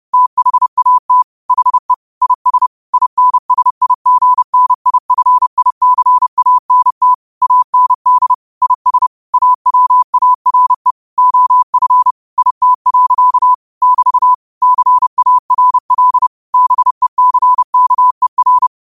Morse code, heard as a keyed tone, about 30 wpm, 1000 Hz. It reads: THAT HE IS INSIGNIFICANT AND IS AWARE OF IT. <BT> CARL BECKER
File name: qotdfiles / 30wpm